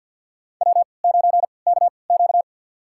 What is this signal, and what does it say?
Morse code, heard as a keyed tone, about 47 wpm, spaced as for 28 wpm, 700 Hz. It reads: W9KQ
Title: USA Call Signs - 4 Characters - 3x Character Spacing - 50wpm